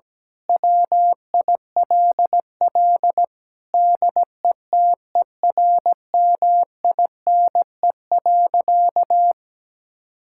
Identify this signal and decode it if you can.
Morse code, heard as a keyed tone, about 17 wpm, 700 Hz. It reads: WILL DETERMINE.
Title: Sentences from Top 1000 Words - Review Part 5 - 17wpm